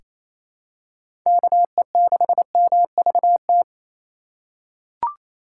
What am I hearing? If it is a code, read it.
Morse code, heard as a keyed tone, about 28 wpm, 700 Hz. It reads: KE6MVT
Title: USA Call Signs - 6 Characters - 6x Character Spacing - 28wpm